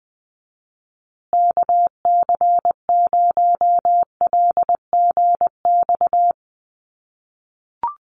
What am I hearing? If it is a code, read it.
Morse code, heard as a keyed tone, about 20 wpm, 700 Hz. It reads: KC0LGX
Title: USA Call Signs Repeated 2x 20wpm